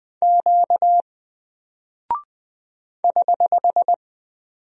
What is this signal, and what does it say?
Morse code, heard as a keyed tone, about 20 wpm, 700 Hz. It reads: Q <HH>
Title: Single Character (Letter:Number:Punctuation:Prosign) 20wpm